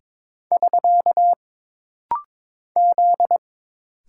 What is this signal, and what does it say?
Morse code, heard as a keyed tone, about 22 wpm, 700 Hz. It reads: <SK> Z
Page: This rapid-fire timing is geared to push your ICR (Instant Character Recognition) skills.